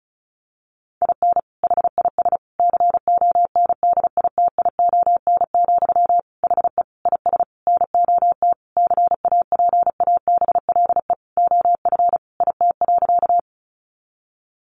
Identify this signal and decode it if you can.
Morse code, heard as a keyed tone, about 34 wpm, 700 Hz. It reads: IN HIS CONDITION, HE IS NOT CAPABLE OF IT.
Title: Sentences from Top 1200 Words 35wpm